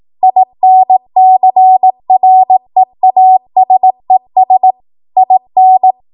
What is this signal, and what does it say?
Morse code, heard as a keyed tone, about 18 wpm, 750 Hz. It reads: INCREASES IN
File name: Archive / 18 WPM / 18WPM